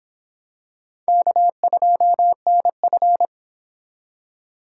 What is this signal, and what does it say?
Morse code, heard as a keyed tone, about 26 wpm, 700 Hz. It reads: K2NF